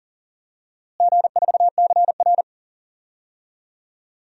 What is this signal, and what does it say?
Morse code, heard as a keyed tone, about 40 wpm, 700 Hz. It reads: G4CR
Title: DXCC-100 Call Signs - 4 Characters - Repeated 3x - 40wpm